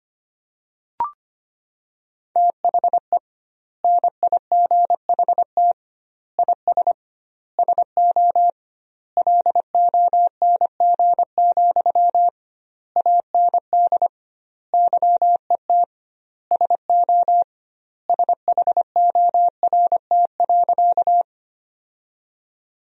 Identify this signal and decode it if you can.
Morse code, heard as a keyed tone, about 25 wpm, 700 Hz. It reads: THE NIGHT IS SO LONG, AND YET SO SHORT.
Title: 2x Word-Spacing - Sentences from Top 500 Words - 25wpm